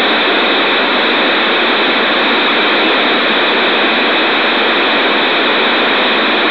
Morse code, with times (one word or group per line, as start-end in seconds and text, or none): none